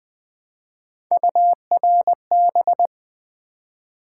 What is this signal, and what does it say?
Morse code, heard as a keyed tone, about 20 wpm, 700 Hz. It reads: URB
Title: Build A Word - 1 Letter At A Time - 5-Letter Words - 20wpm